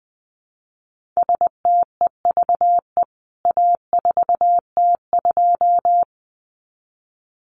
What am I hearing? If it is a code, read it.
Morse code, heard as a keyed tone, about 20 wpm, 700 Hz. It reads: STEVE A4T2
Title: CWT Exchange 20wpm